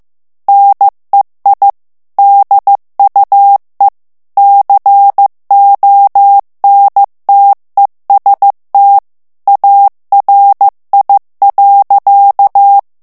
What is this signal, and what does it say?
Morse code, heard as a keyed tone, about 15 wpm, 800 Hz. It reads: NEI DUE CONTEST ARI.
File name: morse mp3 / ITALIEN texte clair 2 vitesse 14 mots minute